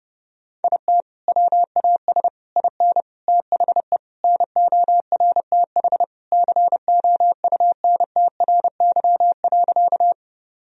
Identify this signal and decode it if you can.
Morse code, heard as a keyed tone, about 30 wpm, 700 Hz. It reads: IT WAS IN THE NORTH COUNTRY.
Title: Sentences from Top 1000 Words - Review Part 1 - 30wpm